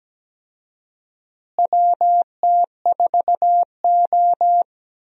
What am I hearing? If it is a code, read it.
Morse code, heard as a keyed tone, about 17 wpm, 700 Hz. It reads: WT4O
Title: USA Call Signs - 4 Characters - Repeated 2x - 17wpm